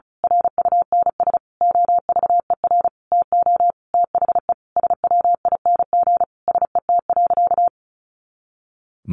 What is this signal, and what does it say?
Morse code, heard as a keyed tone, about 34 wpm, 700 Hz. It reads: RUNS OVER TO THE SWING SET.
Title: Additional Easy Phrases 35wpm